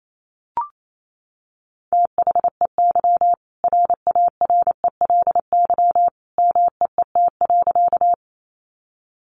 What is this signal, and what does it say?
Morse code, heard as a keyed tone, about 28 wpm, 700 Hz. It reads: THEY RARELY MEET.
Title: Sentences from Top 1400 Words 28wpm